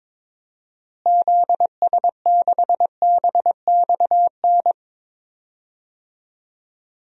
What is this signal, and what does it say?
Morse code, heard as a keyed tone, about 22 wpm, 700 Hz. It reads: ZS6BXN
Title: DXCC-100 Call Signs - 6 Characters - Repeated 3x - 22wpm